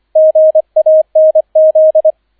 Morse code, heard as a keyed tone, about 25 wpm, 630 Hz, with radio noise. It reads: GANZ